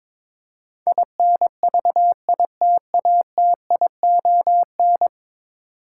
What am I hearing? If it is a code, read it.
Morse code, heard as a keyed tone, about 22 wpm, 700 Hz. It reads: INVITATION
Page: morse-code-ninja-practice-22wpm